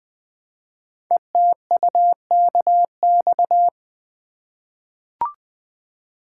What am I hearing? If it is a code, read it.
Morse code, heard as a keyed tone, about 20 wpm, 700 Hz. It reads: ETUKX
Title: Code Groups - 5 Letters - 20wpm